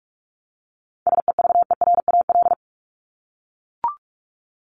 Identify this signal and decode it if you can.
Morse code, heard as a keyed tone, about 44 wpm, 700 Hz. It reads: SEVERAL
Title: Top 500 Words 45wpm